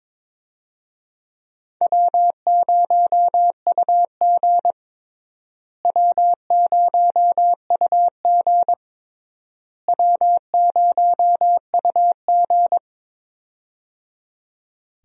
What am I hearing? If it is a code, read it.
Morse code, heard as a keyed tone, about 22 wpm, 700 Hz. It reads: W0UG W0UG W0UG
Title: USA Call Signs - 4 Characters - Repeated 3x - 22wpm